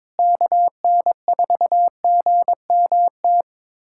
Morse code, heard as a keyed tone, about 22 wpm, 700 Hz. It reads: KN4GMT
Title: USA Callsign DE Callsign - 6 Characters - 22wpm